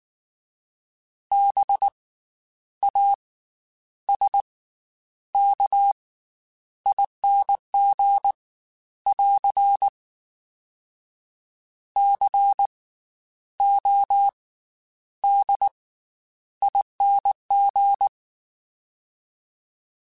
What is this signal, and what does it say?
Morse code, heard as a keyed tone, about 19 wpm, 800 Hz. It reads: B A S K ING <AR> C O D ING
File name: Special INGs / Special INGs-mp3s / Slow ing @18